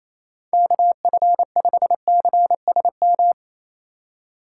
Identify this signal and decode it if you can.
Morse code, heard as a keyed tone, about 35 wpm, 700 Hz. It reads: KF5CSM